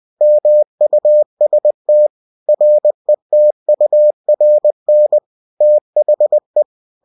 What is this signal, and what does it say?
Morse code, heard as a keyed tone, about 20 wpm, 600 Hz. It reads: MUST RETURN THE
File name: cw-books / winnie-the-pooh / winnie-20wpm-page-0119